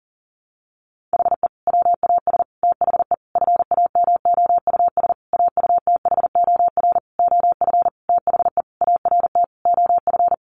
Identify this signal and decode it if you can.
Morse code, heard as a keyed tone, about 40 wpm, 700 Hz. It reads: HE WAS THE FAMOUS AUTHOR OF THE ART OF
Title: Sentences from Top 1500 Words 40wpm